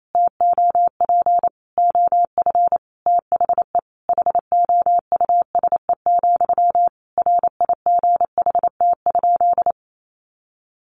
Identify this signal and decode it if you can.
Morse code, heard as a keyed tone, about 28 wpm, 700 Hz. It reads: TOP OF THE HOUSE, RIGHT?
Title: Sentences from Top 1200 Words 28wpm